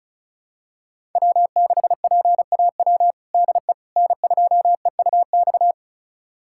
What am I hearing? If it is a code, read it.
Morse code, heard as a keyed tone, about 34 wpm, 700 Hz. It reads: W6PAW DE N2EUX
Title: USA Callsign DE Callsign - 5 Characters - 35wpm